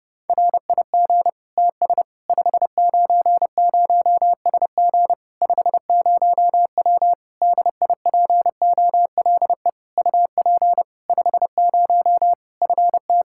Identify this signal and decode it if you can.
Morse code, heard as a keyed tone, about 32 wpm, 700 Hz. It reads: RIG TS 590SG 50W DIPOLE UP 50 FT
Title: Practice QSO - Exchange 2 - Info - 30wpm